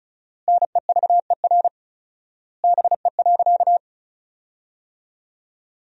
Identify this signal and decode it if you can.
Morse code, heard as a keyed tone, about 36 wpm, 700 Hz. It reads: NEVER BE.
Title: 4x Word-Spacing - Sentences from Top 300 Words - 35wpm